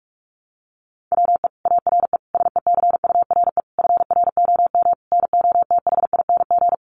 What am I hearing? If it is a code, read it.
Morse code, heard as a keyed tone, about 44 wpm, 700 Hz. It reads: WE ARE SECURE FROM NOTHING
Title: Sentences from Top 1200 Words 45wpm